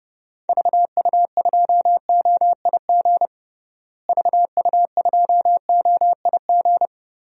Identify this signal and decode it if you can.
Morse code, heard as a keyed tone, about 30 wpm, 700 Hz. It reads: VU2OIG VU2OIG
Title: DXCC-100 Call Signs - 6 Characters - Repeated 2x - 30wpm